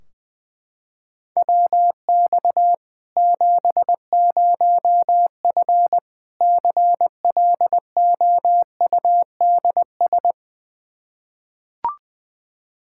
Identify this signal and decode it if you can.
Morse code, heard as a keyed tone, about 20 wpm, 700 Hz. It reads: WX 70F CLOUDS
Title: Practice QSO - Exchange 2 - Info - 20wpm